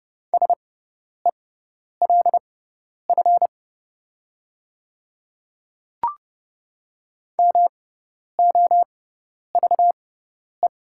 Morse code, heard as a keyed tone, about 30 wpm, spaced as for 10 wpm, 700 Hz. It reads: SELF MOVE
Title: 6x Character Spacing - Top 200 Words - 30wpm